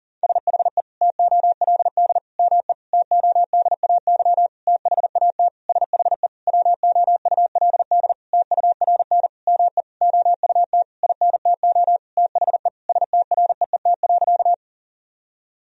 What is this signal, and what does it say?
Morse code, heard as a keyed tone, about 40 wpm, 700 Hz. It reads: SHE TOLD ME TODAY THAT SHE WOULD TURN ME OUT INTO THE STREET.
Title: Sentences from Top 1000 Words - Review Part 4 - 40wpm